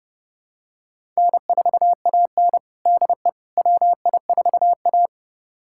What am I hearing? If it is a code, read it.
Morse code, heard as a keyed tone, about 32 wpm, 700 Hz. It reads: N4AN DE WI4A
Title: USA Callsign DE Callsign - 4 Characters - 30wpm